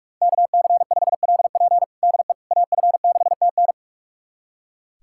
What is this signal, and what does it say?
Morse code, heard as a keyed tone, about 42 wpm, 700 Hz. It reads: KC5LP DE AF6TN
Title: USA Callsign DE Callsign - 5 Characters - 45wpm